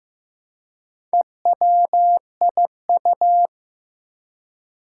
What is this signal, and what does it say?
Morse code, heard as a keyed tone, about 15 wpm, 700 Hz. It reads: EWIU
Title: Four Challenging Letter-Number Combinations 15wpm